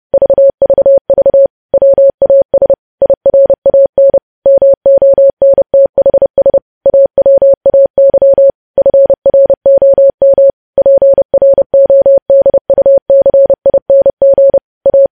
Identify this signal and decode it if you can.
Morse code, heard as a keyed tone, about 30 wpm, 550 Hz. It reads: VVV WAS IRAN MONTHS AWAY FROM PRODUCING A
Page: News Headlines in Morse Code at 30 WPM